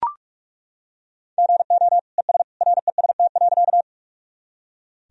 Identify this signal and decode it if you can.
Morse code, heard as a keyed tone, about 41 wpm, 700 Hz. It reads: GO ES REST.
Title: Sentences from Top 500 Words and Abbreviations 45wpm